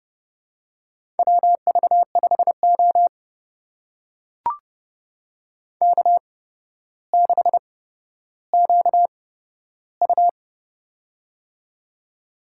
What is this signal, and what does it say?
Morse code, heard as a keyed tone, about 29 wpm, spaced as for 13 wpm, 700 Hz. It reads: WV5O K6QU
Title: USA Call Signs - 4 Characters - 8x Character Spacing - 30wpm